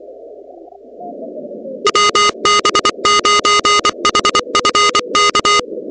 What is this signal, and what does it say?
Morse code, heard as a keyed tone, about 24 wpm, 1240 Hz, with noise. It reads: WB9HFK
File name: contests / 2019-cw-wpx-kt5j-k5pi- -2330 0-16